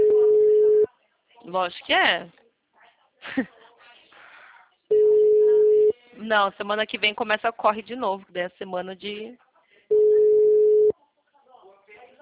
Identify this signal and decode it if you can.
Ringback tone (Europe, 425 Hz), with speech.